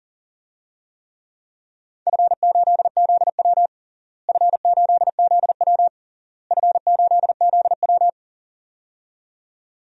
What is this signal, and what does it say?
Morse code, heard as a keyed tone, about 40 wpm, 700 Hz. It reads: F8ZW F8ZW F8ZW
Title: DXCC-100 Call Signs - 4 Characters - Repeated 3x - 40wpm